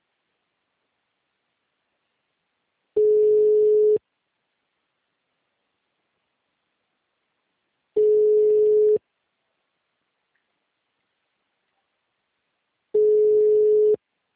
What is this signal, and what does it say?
Ringback tone (Europe, 425 Hz), with speech.